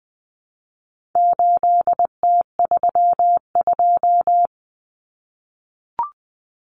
Morse code, heard as a keyed tone, about 20 wpm, 700 Hz. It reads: 8T32
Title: Common Cut Numbers - 4-Digit 20wpm